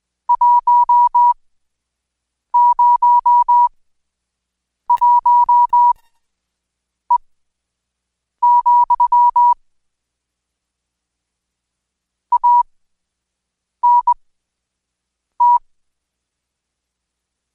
Morse code, heard as a keyed tone, about 21 wpm, spaced as for 6 wpm, 970 Hz. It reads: 101E, ANT
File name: Code Quick/AudioDownld2/Seg109-Day14-DayDrill 14